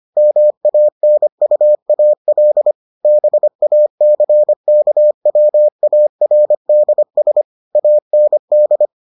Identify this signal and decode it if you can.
Morse code, heard as a keyed tone, about 25 wpm, 600 Hz. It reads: MANUAL BACKWARDS AND